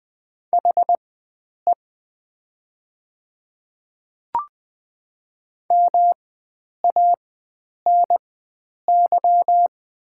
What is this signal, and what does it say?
Morse code, heard as a keyed tone, about 20 wpm, spaced as for 9 wpm, 690 Hz. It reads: HE MANY
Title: 4x Character Spacing - Top 100 Words - 20wpm